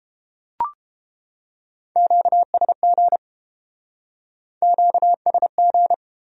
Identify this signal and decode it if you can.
Morse code, heard as a keyed tone, about 30 wpm, 700 Hz. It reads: QSG QSG